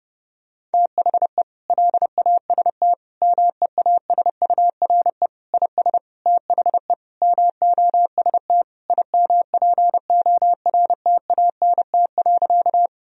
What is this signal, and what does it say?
Morse code, heard as a keyed tone, about 30 wpm, 700 Hz. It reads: THE LAST MEASURE IS THE MOST IMPORTANT.
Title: Sentences from Top 1000 Words - Review Part 2 - 30wpm